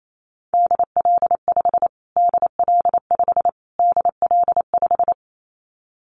Morse code, heard as a keyed tone, about 28 wpm, 700 Hz. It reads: DL5 DL5 DL5
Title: Callsign Prefixes for Top 25 DXCC - Repeated 3x - 28wpm